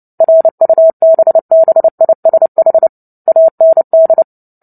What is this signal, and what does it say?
Morse code, heard as a keyed tone, about 30 wpm, 650 Hz. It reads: RUBBISH AND
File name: Book Two / 03-Chapter 3 - 29 WPM